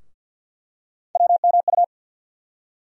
Morse code, heard as a keyed tone, about 49 wpm, 700 Hz. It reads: WMU
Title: USA Call Sign Postfixes - 3 Letters 50wpm